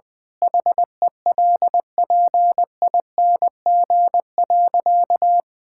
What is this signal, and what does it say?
Morse code, heard as a keyed tone, about 20 wpm, 700 Hz. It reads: HELPING.
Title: Sentences with 3-Word Phrases - 20wpm